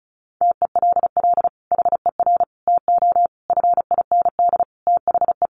Morse code, heard as a keyed tone, about 34 wpm, 700 Hz. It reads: TELL HER TO FIND THE
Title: Sentences from Top 1200 Words 35wpm